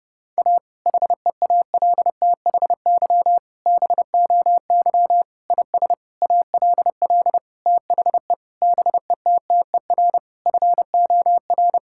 Morse code, heard as a keyed tone, about 30 wpm, 700 Hz. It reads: A HEALTHY BOY IS ALL THE BETTER FOR
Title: Sentences from Top 1000 Words - Review Part 5 - 30wpm